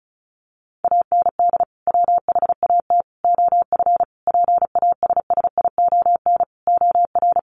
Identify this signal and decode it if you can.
Morse code, heard as a keyed tone, about 34 wpm, 700 Hz. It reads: AND WHAT OF PASSION OR
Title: Sentences from Top 1400 Words 35wpm